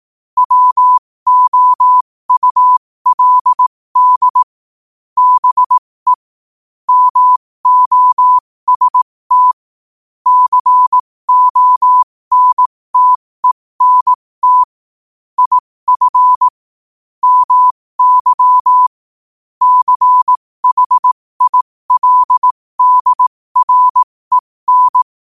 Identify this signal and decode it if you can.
Morse code, heard as a keyed tone, about 18 wpm, spaced as for 15 wpm, 1000 Hz. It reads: WOULD BE MOST CONTENT IF MY CHILDREN
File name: qotdfiles / 15wpm